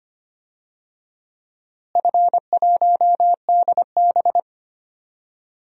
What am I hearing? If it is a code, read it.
Morse code, heard as a keyed tone, about 25 wpm, 700 Hz. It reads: F1DB
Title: DXCC-100 Call Signs - 4 Characters - Repeated 2x - 25wpm